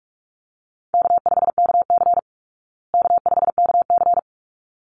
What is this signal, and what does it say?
Morse code, heard as a keyed tone, about 44 wpm, 700 Hz. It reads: K5KC K5KC
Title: Easy Call Signs Repeated 2x - 45wpm